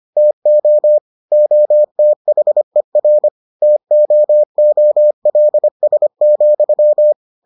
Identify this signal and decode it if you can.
Morse code, heard as a keyed tone, about 25 wpm, 600 Hz. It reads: TO OTHER TOOLS,